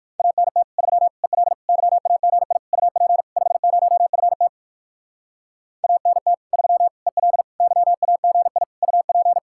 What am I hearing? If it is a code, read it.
Morse code, heard as a keyed tone, about 58 wpm, 700 Hz. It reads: ANT 3 EL YAGI UP 50FT ANT 3 EL YAGI UP